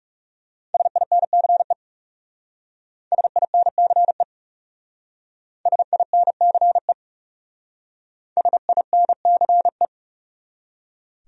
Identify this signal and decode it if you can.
Morse code, heard as a keyed tone, about 38 wpm, 700 Hz. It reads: SINCE SINCE SINCE SINCE